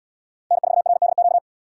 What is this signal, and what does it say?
Morse code, heard as a keyed tone, about 72 wpm, 700 Hz. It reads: N4RDQ